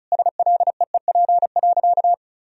Morse code, heard as a keyed tone, about 36 wpm, 700 Hz. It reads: SLEEP.
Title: Sentences from Top 1000 Words - Review Part 2 - 35wpm